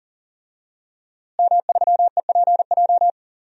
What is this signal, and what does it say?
Morse code, heard as a keyed tone, about 40 wpm, 700 Hz. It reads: M3EPJ
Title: DXCC-100 Call Signs - 5 Characters - Repeated 2x - 40wpm